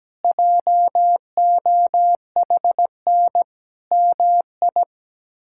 Morse code, heard as a keyed tone, about 17 wpm, 700 Hz. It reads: JOHN MI
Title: K1USN SST QSO Practice - 17wpm